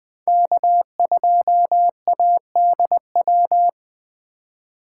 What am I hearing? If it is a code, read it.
Morse code, heard as a keyed tone, about 20 wpm, 700 Hz. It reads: K2ADW